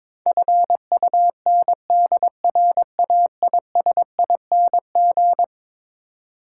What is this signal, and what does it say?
Morse code, heard as a keyed tone, about 22 wpm, 700 Hz. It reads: FUNDRAISING
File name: practice / 22wpm / Eleven Letter Words - 22wpm